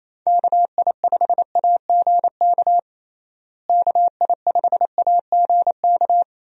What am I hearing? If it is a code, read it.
Morse code, heard as a keyed tone, about 27 wpm, 700 Hz. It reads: KI5AGK KI5AGK
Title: USA Call Signs - 6 Characters - Repeated 2x - 28wpm